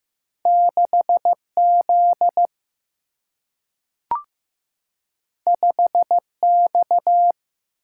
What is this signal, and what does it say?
Morse code, heard as a keyed tone, about 15 wpm, 700 Hz. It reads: 6Z 5X
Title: Single-Letter Single-Number 15wpm